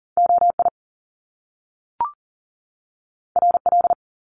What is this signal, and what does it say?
Morse code, heard as a keyed tone, about 40 wpm, 700 Hz. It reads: OI RL
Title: Top Two Letter Combinations in Words 40wpm